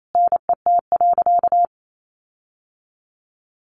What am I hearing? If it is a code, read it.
Morse code, heard as a keyed tone, about 28 wpm, 700 Hz. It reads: NET.
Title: Sentences from Top 1500 Words 28wpm